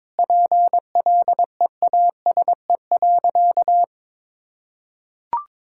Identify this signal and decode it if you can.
Morse code, heard as a keyed tone, about 22 wpm, 700 Hz. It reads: PLEASE.
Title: Sentences from Top 1600 Words 22wpm